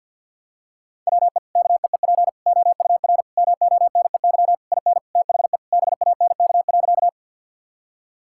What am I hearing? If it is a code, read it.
Morse code, heard as a keyed tone, about 45 wpm, 700 Hz. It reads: WE KEEP OUR MONEY IN THE BANK.
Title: Sentences from Top 1000 Words - Review Part 6 - 50wpm